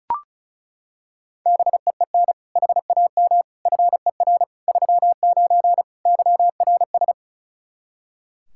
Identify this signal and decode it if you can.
Morse code, heard as a keyed tone, about 36 wpm, 700 Hz. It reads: BEEN HAM FER 39 YRS
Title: Practice QSO - Exchange 3 - More Info - 35wpm